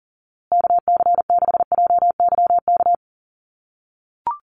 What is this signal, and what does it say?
Morse code, heard as a keyed tone, about 40 wpm, 700 Hz. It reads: KC6JYK
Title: USA Call Signs Repeated 3x 40wpm